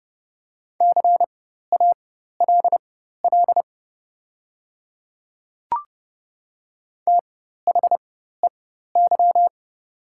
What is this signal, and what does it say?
Morse code, heard as a keyed tone, about 30 wpm, spaced as for 14 wpm, 690 Hz. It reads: CALL THEY
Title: 4x Character Spacing - Top 300 Words - 30wpm